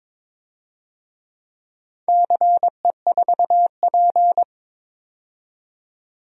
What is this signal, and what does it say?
Morse code, heard as a keyed tone, about 21 wpm, 700 Hz. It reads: CE4P
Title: DXCC-100 Call Signs - 4 Characters - 22wpm